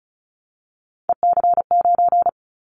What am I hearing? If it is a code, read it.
Morse code, heard as a keyed tone, about 34 wpm, 700 Hz. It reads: EC9
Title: Callsign Prefixes for Top 100 DXCC - 35wpm